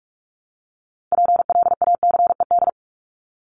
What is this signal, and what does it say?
Morse code, heard as a keyed tone, about 44 wpm, 700 Hz. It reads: PLACED
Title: ED Suffix - Top 100 Words with Base Words 45wpm